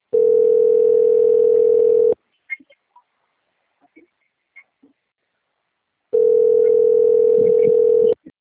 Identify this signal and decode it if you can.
Ringback tone (North America), 440 + 480 Hz.